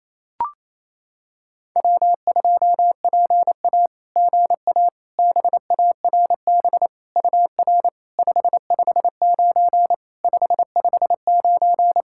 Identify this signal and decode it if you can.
Morse code, heard as a keyed tone, about 28 wpm, 690 Hz. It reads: W2PA GA BARB UR 559 559
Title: SOTA QSO Practice 28wpm